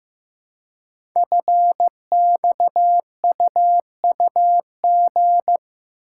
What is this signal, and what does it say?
Morse code, heard as a keyed tone, about 15 wpm, 700 Hz. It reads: FXUUG
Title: Code Groups - 5 Letters - 15wpm